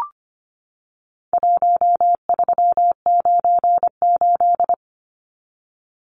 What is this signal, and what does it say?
Morse code, heard as a keyed tone, about 25 wpm, 700 Hz. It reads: 1398
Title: Numbers - Contest Style - 4-Digit 25wpm